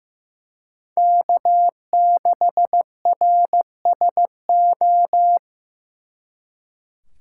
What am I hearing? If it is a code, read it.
Morse code, heard as a keyed tone, about 19 wpm, 700 Hz. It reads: K6RSO
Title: Speed Racing - USA 1x3 Call Signs 15wpm